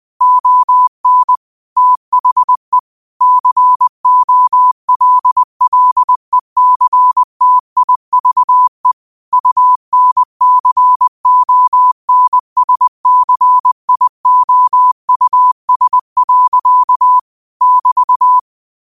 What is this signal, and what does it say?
Morse code, heard as a keyed tone, about 20 wpm, 1000 Hz. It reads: ON THE COLLECTIVE UNCONSCIOUS. <BT>
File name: qotdfiles / 20wpm